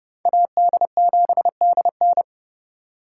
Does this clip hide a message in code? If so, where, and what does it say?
Yes: Morse code from 0.3 to 2.2 s, heard as a keyed tone, about 29 wpm, 700 Hz. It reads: AD7DN